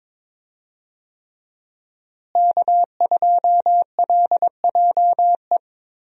Morse code, heard as a keyed tone, about 22 wpm, 700 Hz. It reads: K2LJE
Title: USA Call Signs - 5 Characters - Repeated 3x - 22wpm